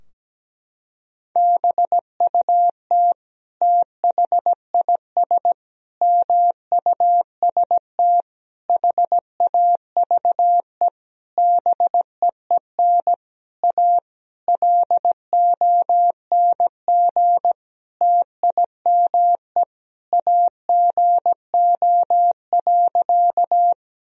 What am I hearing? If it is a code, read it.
Morse code, heard as a keyed tone, about 17 wpm, 700 Hz. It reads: BUT THIS MUST HAVE BEEN A LONG TIME AGO.
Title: Sentences from Top 1000 Words - Review Part 3 - 17wpm